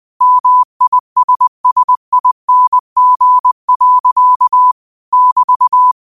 Morse code, heard as a keyed tone, about 20 wpm, 1000 Hz. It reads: MISSING. <BT>
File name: qotdfiles / 20wpm